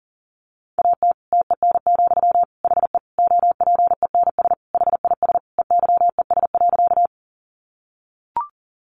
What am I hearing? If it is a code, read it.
Morse code, heard as a keyed tone, about 40 wpm, 700 Hz. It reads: AT TEN, HE OPENS HIS EYES.
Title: Easy Phrases 40wpm